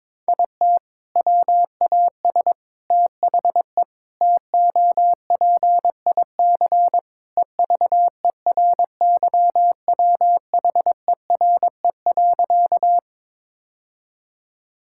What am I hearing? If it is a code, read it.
Morse code, heard as a keyed tone, about 22 wpm, 700 Hz. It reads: IT WAS THE TOPIC EVERYWHERE.
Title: Sentences from Top 1600 Words 22wpm